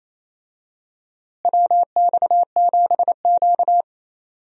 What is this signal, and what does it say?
Morse code, heard as a keyed tone, about 28 wpm, 700 Hz. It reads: WX7Q
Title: USA Call Signs - 4 Characters - Repeated 3x - 28wpm